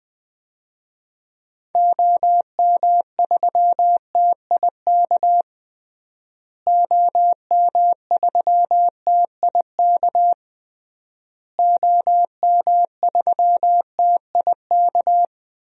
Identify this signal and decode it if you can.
Morse code, heard as a keyed tone, about 20 wpm, 700 Hz. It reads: OM3TIK OM3TIK OM3TIK
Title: DXCC-100 Call Signs - 6 Characters - Repeated 3x - 20wpm